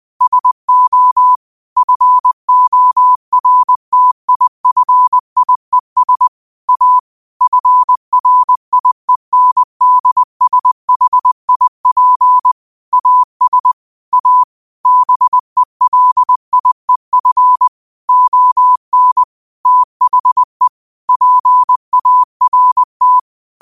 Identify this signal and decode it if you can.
Morse code, heard as a keyed tone, about 20 wpm, 1000 Hz. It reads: SO FORTIFIES A FRIENDSHIP AS A BELIEF ON THE PART
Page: New quotes every day in morse code at 20 Words per minute.